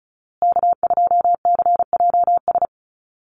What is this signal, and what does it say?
Morse code, heard as a keyed tone, about 34 wpm, 700 Hz. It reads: K2CJS
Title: USA Call Signs Repeated 3x 35wpm